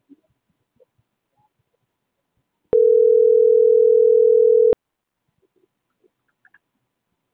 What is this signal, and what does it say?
Ringback tone (North America), 440 + 480 Hz.